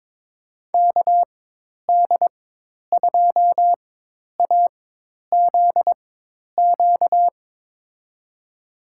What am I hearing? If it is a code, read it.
Morse code, heard as a keyed tone, about 22 wpm, spaced as for 10 wpm, 700 Hz. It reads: KD2AZQ
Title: USA Call Signs - 6 Characters - 4x Character Spacing - 22wpm